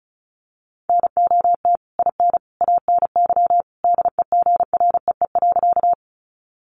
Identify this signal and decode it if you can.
Morse code, heard as a keyed tone, about 34 wpm, 700 Hz. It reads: NOT IN ANY DEGREE.
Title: Sentences from Top 1500 Words 35wpm